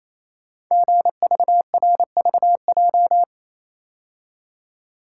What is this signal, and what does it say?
Morse code, heard as a keyed tone, about 28 wpm, 690 Hz. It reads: GVRVJ
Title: Code Groups - 5 Letter:Numbers - 28wpm